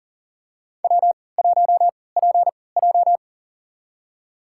Morse code, heard as a keyed tone, about 40 wpm, spaced as for 23 wpm, 700 Hz. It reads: W1PJ
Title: USA Call Signs - 4 Characters - 3x Character Spacing - 40wpm